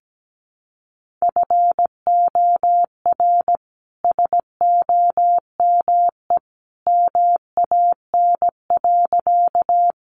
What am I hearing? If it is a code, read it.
Morse code, heard as a keyed tone, about 17 wpm, 700 Hz. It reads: FOR SOME MAN.
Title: Sentences from Top 200 Words 17wpm